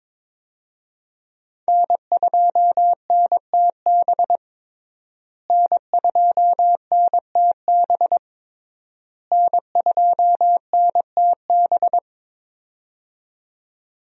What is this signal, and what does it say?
Morse code, heard as a keyed tone, about 22 wpm, 700 Hz. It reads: N2NTB N2NTB N2NTB
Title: USA Call Signs - 5 Characters - Repeated 3x - 22wpm